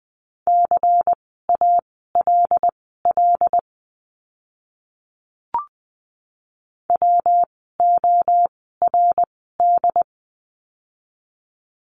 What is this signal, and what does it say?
Morse code, heard as a keyed tone, about 20 wpm, spaced as for 15 wpm, 700 Hz. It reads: CALL WORD
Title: 115 Lesson 19f - Letter w - Words 20wpm